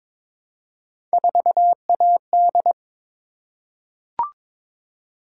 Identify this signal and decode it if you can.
Morse code, heard as a keyed tone, about 23 wpm, 700 Hz. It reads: 4AD
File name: practice / 22wpm / Three Letter-Number Combinations - 22wpm